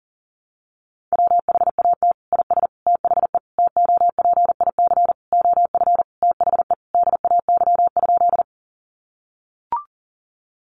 Morse code, heard as a keyed tone, about 40 wpm, 700 Hz. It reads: WHAT IS THE TOPIC OF THE DAY?
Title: Sentences from Top 1200 Words 40wpm